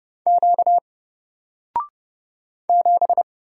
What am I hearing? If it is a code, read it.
Morse code, heard as a keyed tone, about 30 wpm, 700 Hz. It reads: Q 7
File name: Single-Letter-Number-30wpm